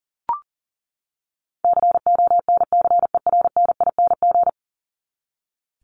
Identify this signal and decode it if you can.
Morse code, heard as a keyed tone, about 40 wpm, 700 Hz. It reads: CONCERNING
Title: ING Suffix - Top 100 Words with Base Words 40wpm